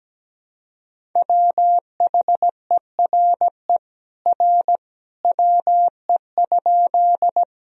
Morse code, heard as a keyed tone, about 17 wpm, 700 Hz. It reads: WHERE R WE?
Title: Sentences from Top 200 Words and Abbreviations 17wpm